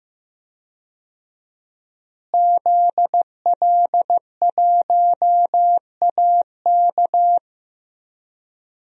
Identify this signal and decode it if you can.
Morse code, heard as a keyed tone, about 15 wpm, 700 Hz. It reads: ZL1AK
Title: DXCC-100 Call Signs - 5 Characters - Repeated 3x - 15wpm